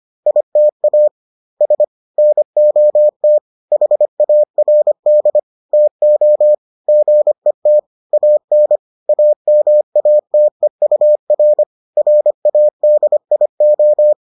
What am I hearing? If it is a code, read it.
Morse code, heard as a keyed tone, about 25 wpm, 600 Hz. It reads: ITA S NOT HARD TO GET AN AMATEUR RADIO